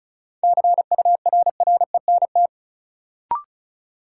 Morse code, heard as a keyed tone, about 36 wpm, 700 Hz. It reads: CURRENT
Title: Sentences from Top 1600 Words - New Word Drill Alt - 35wpm